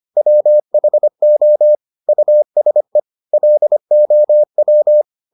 Morse code, heard as a keyed tone, about 25 wpm, 600 Hz. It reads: WHO USE LOW